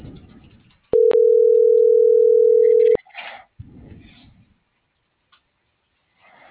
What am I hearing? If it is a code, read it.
Ringback tone (North America), 440 + 480 Hz.